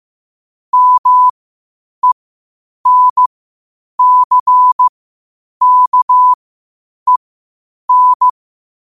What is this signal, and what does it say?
Morse code, heard as a keyed tone, about 15 wpm, spaced as for 8 wpm, 1000 Hz. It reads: MENCKEN
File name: qotdfiles / 8wpm